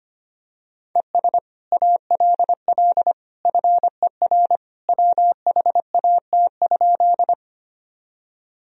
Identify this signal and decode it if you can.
Morse code, heard as a keyed tone, about 25 wpm, 700 Hz. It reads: ES ALL FER WHAT?
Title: Sentences from Top 100 Words and Abbreviations 25wpm